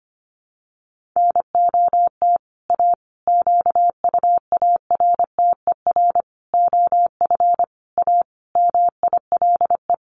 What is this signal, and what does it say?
Morse code, heard as a keyed tone, about 25 wpm, 700 Hz. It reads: NOT A QUARTER OF A MILE
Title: Sentences from Top 1300 Words 25wpm